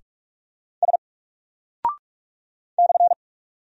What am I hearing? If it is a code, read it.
Morse code, heard as a keyed tone, about 44 wpm, 700 Hz. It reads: S /
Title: Single Character (Letter:Number:Punctuation:Prosign) 45wpm